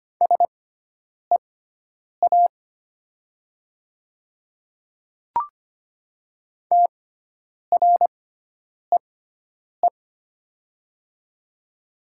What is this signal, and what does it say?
Morse code, heard as a keyed tone, about 25 wpm, spaced as for 9 wpm, 700 Hz. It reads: SEA TREE
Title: 6x Character Spacing - Top 300 Words - 25wpm